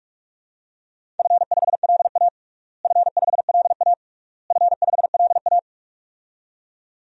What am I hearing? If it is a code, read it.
Morse code, heard as a keyed tone, about 46 wpm, 700 Hz. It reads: F5LA F5LA F5LA
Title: DXCC-100 Call Signs - 4 Characters - Repeated 3x - 45wpm